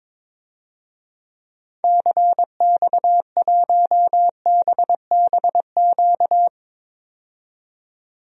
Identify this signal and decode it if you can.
Morse code, heard as a keyed tone, about 22 wpm, 700 Hz. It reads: CX1BBQ
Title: DXCC-100 Call Signs - 6 Characters - 22wpm